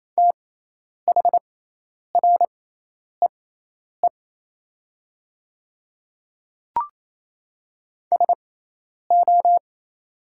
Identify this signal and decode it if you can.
Morse code, heard as a keyed tone, about 27 wpm, spaced as for 10 wpm, 700 Hz. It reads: THREE SO
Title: 6x Character Spacing - Top 200 Words - 28wpm